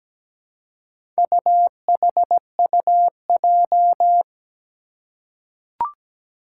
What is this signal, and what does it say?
Morse code, heard as a keyed tone, about 17 wpm, 700 Hz. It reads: UHUJ
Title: Four Challenging Letter-Number Combinations 17wpm